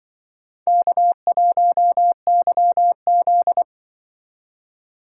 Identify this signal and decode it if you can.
Morse code, heard as a keyed tone, about 25 wpm, 700 Hz. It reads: K1YZ
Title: Speed Racing - USA 1x2 Call Signs 20wpm